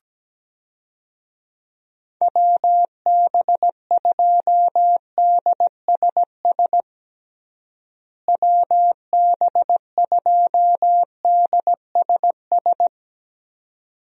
Morse code, heard as a keyed tone, about 17 wpm, 700 Hz. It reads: WB2DSS WB2DSS
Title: USA Call Signs - 6 Characters - Repeated 2x - 17wpm